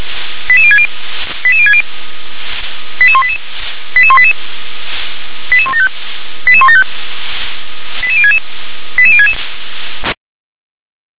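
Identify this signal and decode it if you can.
Five-tone selective call (ZVEI), heard as quick runs of five tones, tones of about 70 ms. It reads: 88060 88060 88188 88188 88175 88175 88066 88066
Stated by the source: ZVEI five-tone selective call received by SDR.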